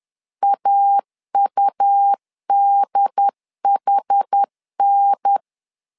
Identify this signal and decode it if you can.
Morse code, heard as a keyed tone, about 10 wpm, 800 Hz. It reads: AUDHN
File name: LEZIONI / 4 Q1 10wpm 2448